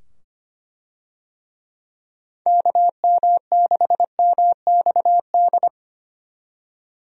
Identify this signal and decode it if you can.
Morse code, heard as a keyed tone, about 25 wpm, 700 Hz. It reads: KM6MXD
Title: USA Call Signs - 6 Characters - 25wpm